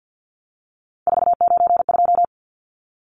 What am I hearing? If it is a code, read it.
Morse code, heard as a keyed tone, about 50 wpm, 700 Hz. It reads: 492
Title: Numbers - 3-Digit 50wpm